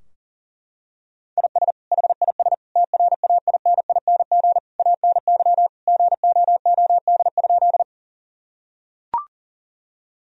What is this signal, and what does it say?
Morse code, heard as a keyed tone, about 40 wpm, 700 Hz. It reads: IS HIS TRAINING ANY GOOD?
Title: Sentences from Top 1000 Words - Review Part 4 - 40wpm